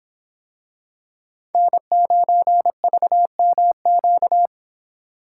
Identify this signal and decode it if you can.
Morse code, heard as a keyed tone, about 25 wpm, 700 Hz. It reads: N9VMQ